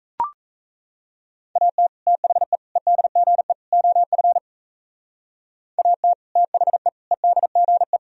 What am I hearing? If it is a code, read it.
Morse code, heard as a keyed tone, about 37 wpm, 700 Hz. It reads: AT THE EDGE OF AT THE EDGE